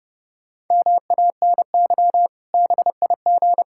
Morse code, heard as a keyed tone, about 30 wpm, 700 Hz. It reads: MANY BIG
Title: Pangram Challenge v2 - 30wpm